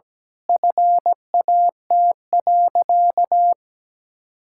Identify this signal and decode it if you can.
Morse code, heard as a keyed tone, about 17 wpm, 700 Hz. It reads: FAT.
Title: Sentences from Top 1000 Words - Review Part 5 - 17wpm